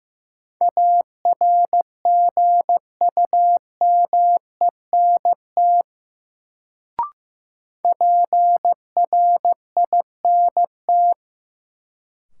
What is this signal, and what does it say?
Morse code, heard as a keyed tone, about 15 wpm, 700 Hz. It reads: ARGUMENT PRINT
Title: Sentences from Top 1400 Words - New Word Drill - 15wpm